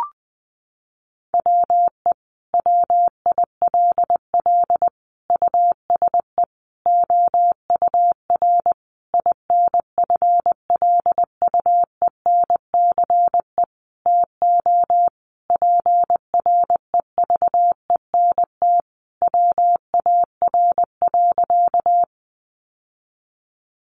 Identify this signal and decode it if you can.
Morse code, heard as a keyed tone, about 19 wpm, 700 Hz. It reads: WE WILL USE OUR INFLUENCE TO PREVENT WAR.
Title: Sentences from Top 1100 Words 20wpm